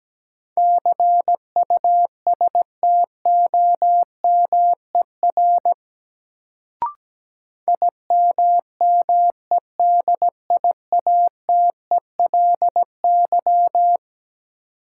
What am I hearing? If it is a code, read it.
Morse code, heard as a keyed tone, about 17 wpm, 700 Hz. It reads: CUSTOMER IMMEDIATELY
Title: Sentences from Top 1100 Words - New Word Drill Alt - 17wpm